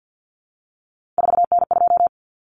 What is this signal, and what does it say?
Morse code, heard as a keyed tone, about 50 wpm, 700 Hz. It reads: 4N2
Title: Common Cut Numbers - 3-Digit 50wpm